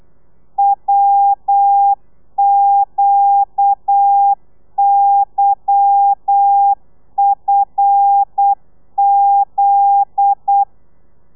Morse code, heard as a keyed tone, about 8 wpm, 800 Hz. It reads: WQYFZ
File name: Lez11